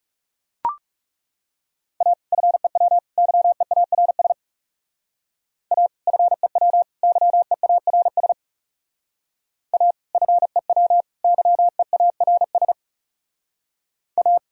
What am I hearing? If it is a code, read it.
Morse code, heard as a keyed tone, about 36 wpm, 700 Hz. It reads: A FEW YEARS A FEW YEARS A FEW YEARS A
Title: Speed-Racing - Sets of 3 Words 30wpm